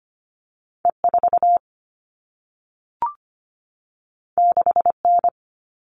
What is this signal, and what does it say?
Morse code, heard as a keyed tone, about 25 wpm, 700 Hz. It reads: E4 6N
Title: All Cut Numbers - 2-Digit 25wpm